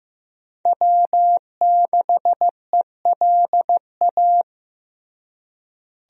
Morse code, heard as a keyed tone, about 15 wpm, 700 Hz. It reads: W6ELA
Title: USA Call Signs - 5 Characters - Repeated 3x - 15wpm